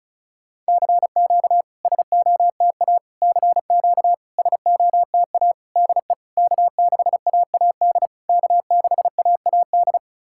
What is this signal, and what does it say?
Morse code, heard as a keyed tone, about 34 wpm, 660 Hz. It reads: CQ SOTA CQ SOTA DE K6AAD K6AAD
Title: SOTA QSO Practice 35wpm